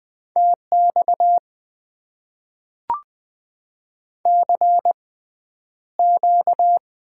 Morse code, heard as a keyed tone, about 20 wpm, spaced as for 10 wpm, 700 Hz. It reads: TX CQ
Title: Top 90 QSO Elements - 6x Character Spacing - 20wpm